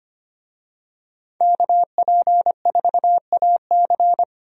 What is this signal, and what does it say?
Morse code, heard as a keyed tone, about 25 wpm, 700 Hz. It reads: KP4AC
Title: USA Call Signs - 5 Characters - Repeated 3x - 25wpm